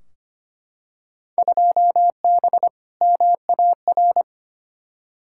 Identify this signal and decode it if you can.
Morse code, heard as a keyed tone, about 25 wpm, 700 Hz. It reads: 2B MAR
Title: Field Day Exchanges Encore - 25wpm